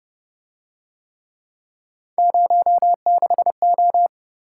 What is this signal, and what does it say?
Morse code, heard as a keyed tone, about 28 wpm, 700 Hz. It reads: 06O